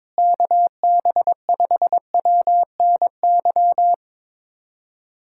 Morse code, heard as a keyed tone, about 22 wpm, 700 Hz. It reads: KB5WNY
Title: USA Callsign DE Callsign 22wpm